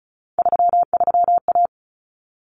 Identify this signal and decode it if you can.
Morse code, heard as a keyed tone, about 34 wpm, 700 Hz. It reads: 33A
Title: Common Cut Numbers - 3-Digit 35wpm